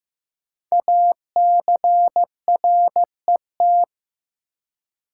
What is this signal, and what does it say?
Morse code, heard as a keyed tone, about 15 wpm, 700 Hz. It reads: ACRET
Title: Code Groups - 5 Letters - 15wpm